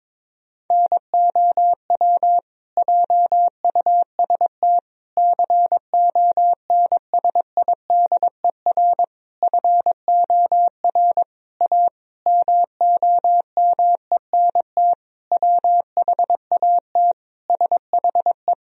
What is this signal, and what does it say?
Morse code, heard as a keyed tone, about 21 wpm, 700 Hz. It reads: NOW JUST CONSIDER FOR A MOMENT WHAT SHE
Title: Sentences from Top 1000 Words - Review Part 4 - 22wpm